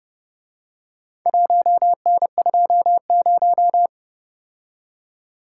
Morse code, heard as a keyed tone, about 29 wpm, 700 Hz. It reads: 1N20